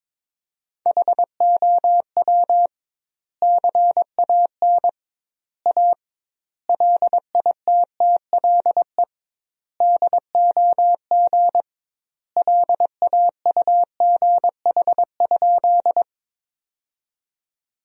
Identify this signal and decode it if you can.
Morse code, heard as a keyed tone, about 22 wpm, 700 Hz. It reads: HOW CAN A LITTLE DOG LAUGH?
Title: 2x Word-Spacing - Sentences from Top 500 Words - 22wpm